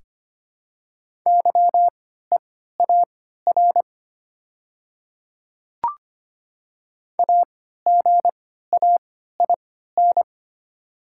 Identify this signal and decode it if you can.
Morse code, heard as a keyed tone, about 25 wpm, spaced as for 14 wpm, 700 Hz. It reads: YEAR AGAIN
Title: 3x Character Spacing - Top 200 Words - 25wpm